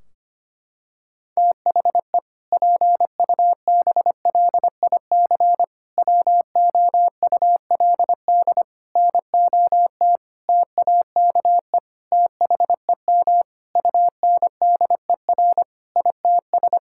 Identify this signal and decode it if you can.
Morse code, heard as a keyed tone, about 25 wpm, 700 Hz. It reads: THE PUBLIC WOULD NOT TAKE THEM UNDER ITS
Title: Sentences from Top 1600 Words 25wpm